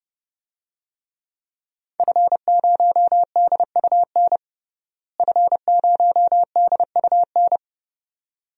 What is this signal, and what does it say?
Morse code, heard as a keyed tone, about 30 wpm, 700 Hz. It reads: F0DUN F0DUN
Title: DXCC-100 Call Signs - 5 Characters - Repeated 2x - 30wpm